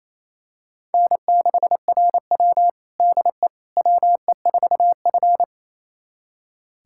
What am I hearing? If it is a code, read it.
Morse code, heard as a keyed tone, about 28 wpm, 700 Hz. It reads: N6RW DE WE4F
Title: USA Callsign DE Callsign - 4 Characters - 28wpm